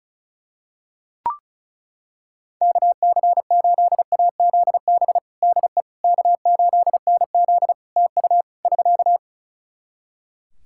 Morse code, heard as a keyed tone, about 36 wpm, 700 Hz. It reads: KC8AZB DE K8NZ TU <SK>
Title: Practice QSO - Exchange 4 - Ending - 35wpm